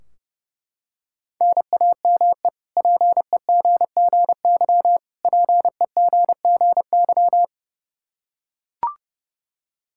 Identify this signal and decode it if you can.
Morse code, heard as a keyed tone, about 32 wpm, 700 Hz. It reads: NAME PEGGY PEGGY
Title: K1USN SST QSO Practice 30wpm